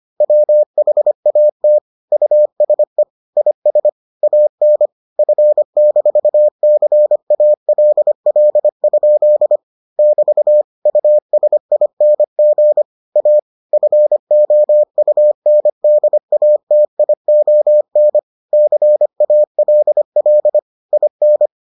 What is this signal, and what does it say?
Morse code, heard as a keyed tone, about 25 wpm, 600 Hz. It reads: WHAT USE IS AN F-CALL? <BT> USING A FOUNDATION CALL IN